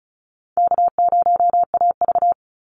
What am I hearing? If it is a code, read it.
Morse code, heard as a keyed tone, about 34 wpm, 700 Hz. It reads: K0AV
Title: Call Signs - 35wpm